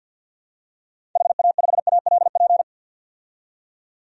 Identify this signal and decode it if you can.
Morse code, heard as a keyed tone, about 52 wpm, 700 Hz. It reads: HA5ALP
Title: DXCC-100 Call Signs - 6 Characters - 50wpm